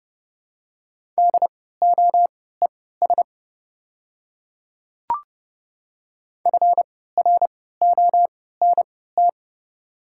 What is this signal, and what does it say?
Morse code, heard as a keyed tone, about 30 wpm, spaced as for 17 wpm, 700 Hz. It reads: DOES FRONT
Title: 3x Character Spacing - Top 500 Words - 30wpm